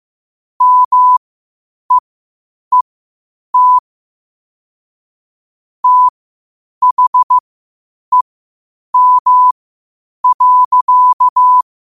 Morse code, heard as a keyed tone, about 15 wpm, spaced as for 8 wpm, 1000 Hz. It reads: MEET THEM.
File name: qotdfiles / 8wpm